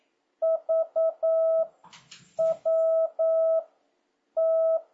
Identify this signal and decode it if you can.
Morse code, heard as a keyed tone, about 9 wpm, 650 Hz. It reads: V W T